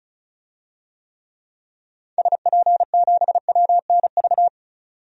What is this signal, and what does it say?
Morse code, heard as a keyed tone, about 34 wpm, 700 Hz. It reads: SP7WNV
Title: DXCC-100 Call Signs - 6 Characters - 35wpm